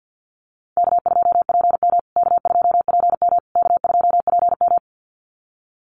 Morse code, heard as a keyed tone, about 48 wpm, 700 Hz. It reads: K2PM K2PM K2PM
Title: USA Call Signs Repeated 3x 50wpm